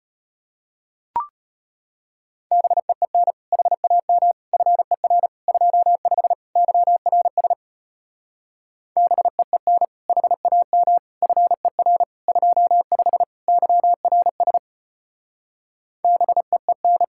Morse code, heard as a keyed tone, about 33 wpm, 700 Hz. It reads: BEEN HAM FER 25 YRS BEEN HAM FER 25 YRS BEEN